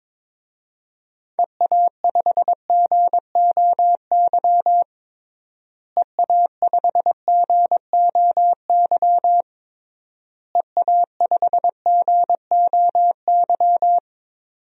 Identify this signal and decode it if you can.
Morse code, heard as a keyed tone, about 22 wpm, 700 Hz. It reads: EA5GOY EA5GOY EA5GOY
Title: DXCC-100 Call Signs - 6 Characters - Repeated 3x - 22wpm